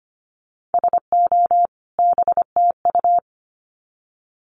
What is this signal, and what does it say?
Morse code, heard as a keyed tone, about 25 wpm, 700 Hz. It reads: SO BTU
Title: Common QSO Phrases 25wpm